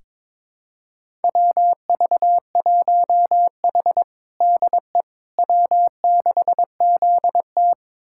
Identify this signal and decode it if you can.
Morse code, heard as a keyed tone, about 22 wpm, 700 Hz. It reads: WV1H DE W6ZT
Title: USA Callsign DE Callsign - 4 Characters - 22wpm